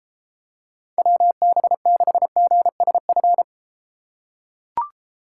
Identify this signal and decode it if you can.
Morse code, heard as a keyed tone, about 32 wpm, 700 Hz. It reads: WB6GSF